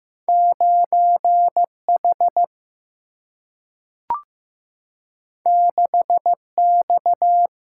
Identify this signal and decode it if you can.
Morse code, heard as a keyed tone, about 15 wpm, 700 Hz. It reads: 9 H 6 X
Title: Single-Letter Single-Number 15wpm